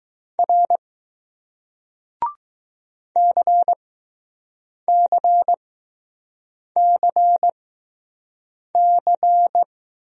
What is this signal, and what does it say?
Morse code, heard as a keyed tone, about 20 wpm, 700 Hz. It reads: R C C C C